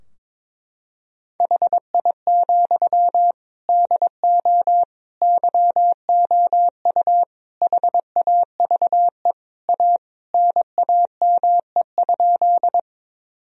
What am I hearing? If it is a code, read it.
Morse code, heard as a keyed tone, about 22 wpm, 700 Hz. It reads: HI, DO YOU HAVE A NAME?
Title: Sentences from Top 1000 Words - Review Part 3 - 22wpm